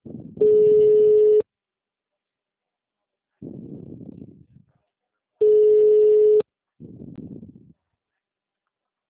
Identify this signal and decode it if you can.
Ringback tone (Europe, 425 Hz).